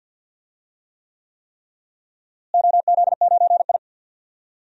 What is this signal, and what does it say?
Morse code, heard as a keyed tone, about 47 wpm, 700 Hz. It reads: OZ9I
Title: DXCC-100 Call Signs - 4 Characters - Repeated 3x - 50wpm